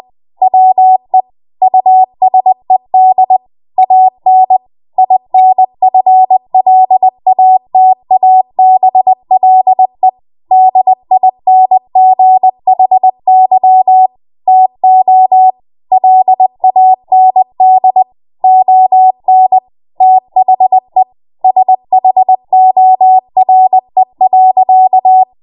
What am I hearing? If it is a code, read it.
Morse code, heard as a keyed tone, about 20 wpm, 750 Hz. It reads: WE USED AN INFLATABLE DINGHY TO LAND ON THE SHORE.
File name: Morse / 20WPM